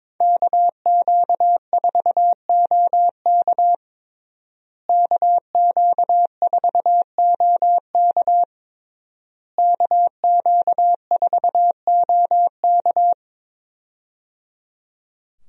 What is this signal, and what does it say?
Morse code, heard as a keyed tone, about 22 wpm, 700 Hz. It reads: KQ4OK KQ4OK KQ4OK
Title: USA Call Signs - 5 Characters - Repeated 3x - 22wpm